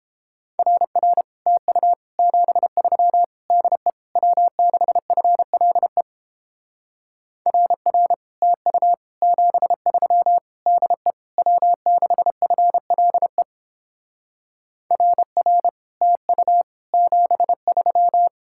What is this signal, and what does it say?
Morse code, heard as a keyed tone, about 29 wpm, 700 Hz. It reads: RR TU 73 DE W6FLE RR TU 73 DE W6FLE RR TU 73